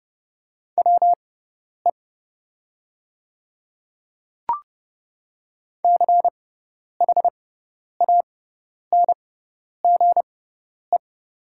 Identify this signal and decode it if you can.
Morse code, heard as a keyed tone, about 30 wpm, spaced as for 10 wpm, 700 Hz. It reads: WE CHANGE
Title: 6x Character Spacing - Top 200 Words - 30wpm